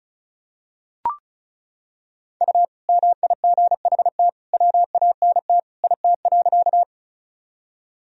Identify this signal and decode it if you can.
Morse code, heard as a keyed tone, about 36 wpm, 700 Hz. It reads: U MIGHT WANT IT.
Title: Sentences from Top 300 Words and Abbreviations 35wpm